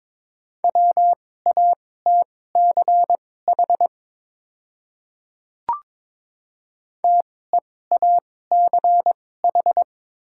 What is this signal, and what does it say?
Morse code, heard as a keyed tone, about 22 wpm, spaced as for 16 wpm, 700 Hz. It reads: WATCH TEACH
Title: 2x Character Spacing - Top 500 Words - 22wpm